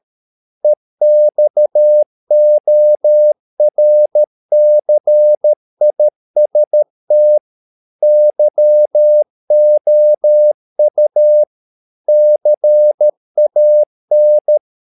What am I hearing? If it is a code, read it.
Morse code, heard as a keyed tone, about 13 wpm, 600 Hz. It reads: EXORCIST YOU CAN